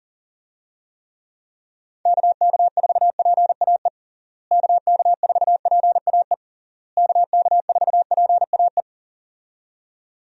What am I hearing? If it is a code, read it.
Morse code, heard as a keyed tone, about 40 wpm, 700 Hz. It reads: KK4PAE KK4PAE KK4PAE
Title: USA Call Signs - 6 Characters - Repeated 3x - 40wpm